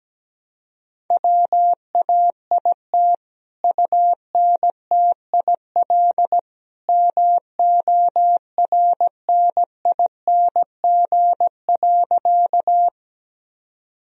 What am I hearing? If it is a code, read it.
Morse code, heard as a keyed tone, about 17 wpm, 700 Hz. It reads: WAIT UNTIL MORNING.
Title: Sentences from Top 1000 Words - Review Part 1 - 17wpm